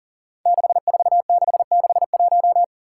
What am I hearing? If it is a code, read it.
Morse code, heard as a keyed tone, about 40 wpm, 700 Hz. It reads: 64661
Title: Five Challenging Letter-Number Combinations 40wpm